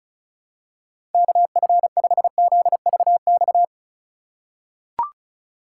Morse code, heard as a keyed tone, about 36 wpm, 700 Hz. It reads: KF5ZVX
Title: USA Call Signs - 3x Character Spacing - 35wpm